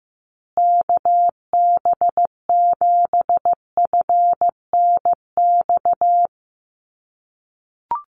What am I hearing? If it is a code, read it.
Morse code, heard as a keyed tone, about 15 wpm, 700 Hz. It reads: KB7FNX
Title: USA Call Signs Repeated 2x 15wpm